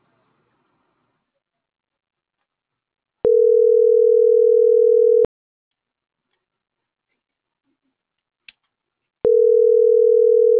Ringback tone (North America), 440 + 480 Hz.